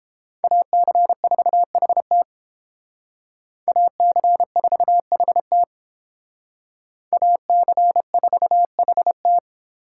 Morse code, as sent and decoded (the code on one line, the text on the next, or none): .- -.-. ....- .... - / .- -.-. ....- .... - / .- -.-. ....- .... -
AC4HT AC4HT AC4HT